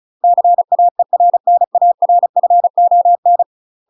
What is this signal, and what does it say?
Morse code, heard as a keyed tone, about 34 wpm, 700 Hz. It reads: CAERNARFON
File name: UK MASTER QSOs TRAINING / 35WPM / UK MASTER QSOs 0025